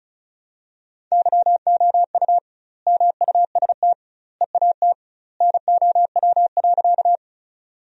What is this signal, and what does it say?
Morse code, heard as a keyed tone, about 34 wpm, 700 Hz. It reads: YOU MUST EAT NOW.
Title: 2x Word-Spacing - Sentences from Top 300 Words - 35wpm